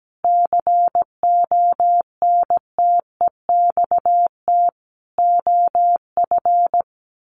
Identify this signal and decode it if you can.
Morse code, heard as a keyed tone, about 17 wpm, 700 Hz. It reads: CONTEXT OF
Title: Sets of 4 Words 17wpm